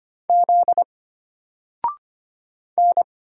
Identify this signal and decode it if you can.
Morse code, heard as a keyed tone, about 25 wpm, 700 Hz. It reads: Z N
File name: Single-Letter-Number-25wpm